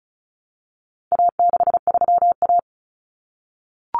Morse code, heard as a keyed tone, about 34 wpm, 700 Hz. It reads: A63A
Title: Common Cut Numbers - 4-Digit 35wpm